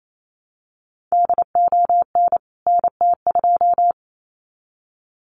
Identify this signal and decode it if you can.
Morse code, heard as a keyed tone, about 28 wpm, 700 Hz. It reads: DON NT2
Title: CWT Exchange 28wpm